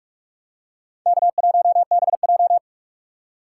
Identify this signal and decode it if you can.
Morse code, heard as a keyed tone, about 46 wpm, 700 Hz. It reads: K1BJ
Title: USA Call Signs - 3x Character Spacing - 45wpm